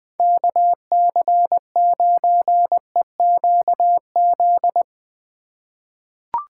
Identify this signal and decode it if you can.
Morse code, heard as a keyed tone, about 20 wpm, 700 Hz. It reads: KC9EQZ
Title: USA Call Signs - 6 Characters - 3x Character Spacing - 20wpm